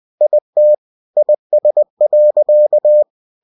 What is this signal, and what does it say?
Morse code, heard as a keyed tone, about 20 wpm, 600 Hz. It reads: IT IS.
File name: cw-books / winnie-the-pooh / winnie-20wpm-page-0057